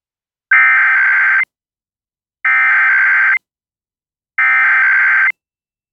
Emergency Alert System (SAME) tones (header bursts), heard as harsh buzzing bursts. It is heard from 0.5 to 5.3 s.